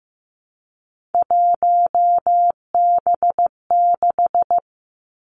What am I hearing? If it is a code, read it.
Morse code, heard as a keyed tone, about 15 wpm, 700 Hz. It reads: 1B6
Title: Three Challenging Letter-Number Combinations 15wpm